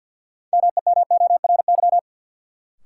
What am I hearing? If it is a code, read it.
Morse code, heard as a keyed tone, about 53 wpm, 700 Hz. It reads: MEMORY
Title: Sentences from Top 1300 Words - New Word Drill Alt - 50wpm